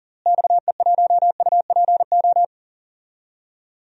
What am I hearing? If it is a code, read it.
Morse code, heard as a keyed tone, about 40 wpm, 700 Hz. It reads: XE1UPO
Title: DXCC-100 Call Signs - 6 Characters - Repeated 3x - 40wpm